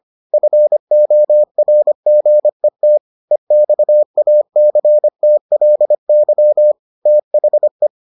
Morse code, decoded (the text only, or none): FORGET EXACTLY THE